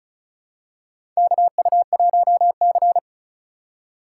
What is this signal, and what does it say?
Morse code, heard as a keyed tone, about 36 wpm, 700 Hz. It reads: KU1C
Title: USA Call Signs - 4x Character Spacing - 35wpm